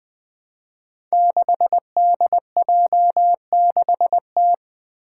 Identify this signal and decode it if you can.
Morse code, heard as a keyed tone, about 20 wpm, 700 Hz. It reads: 6DJ6T
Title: Five Challenging Letter-Number Combinations 20wpm